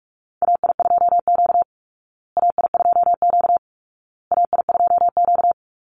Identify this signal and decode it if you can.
Morse code, heard as a keyed tone, about 44 wpm, 700 Hz. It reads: AI2Q AI2Q AI2Q
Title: Call Signs Repeated 2x - 45wpm